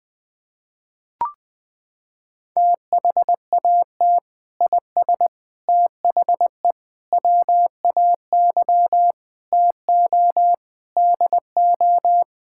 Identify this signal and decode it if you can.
Morse code, heard as a keyed tone, about 19 wpm, 700 Hz. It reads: THAT IS THE WAY TO DO
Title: Sentences with 3-Word Phrases - 20wpm